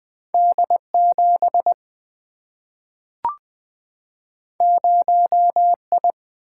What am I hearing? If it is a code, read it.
Morse code, heard as a keyed tone, about 20 wpm, 700 Hz. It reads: D 7 0 I
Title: Single-Letter Single-Number 20wpm